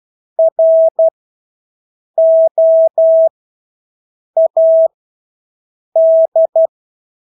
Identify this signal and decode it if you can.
Morse code, heard as a keyed tone, about 12 wpm, spaced as for 6 wpm, 650 Hz. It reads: ROAD
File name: Engelske ord 12-6 wpm